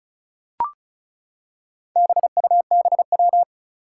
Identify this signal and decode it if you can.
Morse code, heard as a keyed tone, about 36 wpm, 700 Hz. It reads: BUBW
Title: Four Challenging Letter-Number Combinations 35wpm